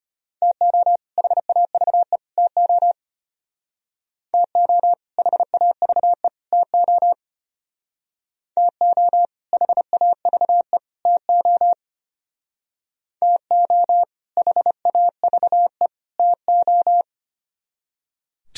Morse code, heard as a keyed tone, about 33 wpm, 700 Hz. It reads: TO HAVE TO TO HAVE TO TO HAVE TO TO HAVE TO